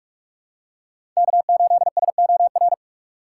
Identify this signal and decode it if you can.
Morse code, heard as a keyed tone, about 46 wpm, 700 Hz. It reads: K8SOR
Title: USA Call Signs - 5 Characters - 4x Character Spacing - 45wpm